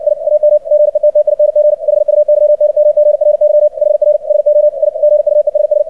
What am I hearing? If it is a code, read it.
Morse code, heard as a keyed tone, about 60 wpm, 600 Hz, with radio noise. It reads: I AM GETTING HUYNMGRY HR SO I GUESS
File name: archived / CRSnet